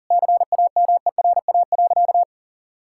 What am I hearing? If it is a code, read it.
Morse code, heard as a keyed tone, about 40 wpm, 700 Hz. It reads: CAMERA.
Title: Sentences from Top 1000 Words - Review Part 5 - 40wpm